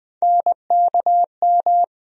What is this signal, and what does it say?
Morse code, heard as a keyed tone, about 20 wpm, 700 Hz. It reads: NKM
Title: USA Call Sign Postfixes - 3 Letters 20wpm